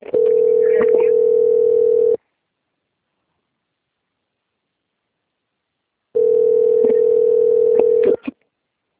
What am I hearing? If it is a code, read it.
Ringback tone (North America), 440 + 480 Hz.
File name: out- - - - - .31367